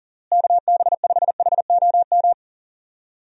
Morse code, heard as a keyed tone, about 40 wpm, 700 Hz. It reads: KB5HOM
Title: USA Call Signs - 6 Characters - Repeated 3x - 40wpm